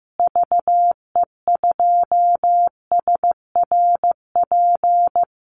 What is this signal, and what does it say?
Morse code, heard as a keyed tone, about 15 wpm, 700 Hz. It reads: VE2SRP
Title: Call Signs Repeated 3x - 15wpm